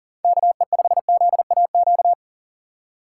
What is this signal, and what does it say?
Morse code, heard as a keyed tone, about 40 wpm, 700 Hz. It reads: KE5ZAQ
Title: USA Call Signs - 6 Characters - 4x Character Spacing - 40wpm